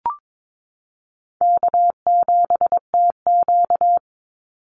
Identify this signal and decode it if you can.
Morse code, heard as a keyed tone, about 22 wpm, 700 Hz. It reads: K7TQ
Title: NA SOTA Chaser Callsigns 22wpm